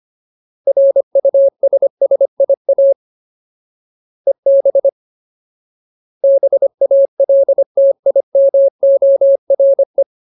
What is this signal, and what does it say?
Morse code, heard as a keyed tone, about 25 wpm, 550 Hz. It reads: RUSSIA EB BALTIMORE
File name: Ultimate Word List / 25WPM Ultimate